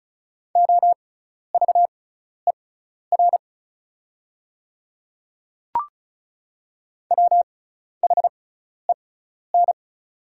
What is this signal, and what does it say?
Morse code, heard as a keyed tone, about 36 wpm, spaced as for 12 wpm, 700 Hz. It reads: OVER WHEN
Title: 6x Character Spacing - Top 100 Words - 35wpm